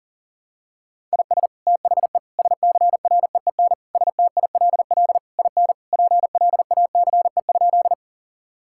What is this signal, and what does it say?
Morse code, heard as a keyed tone, about 40 wpm, 700 Hz. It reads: IS THE SCREEN STILL IN PLACE?
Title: Sentences from Top 1000 Words - Review Part 6 - 40wpm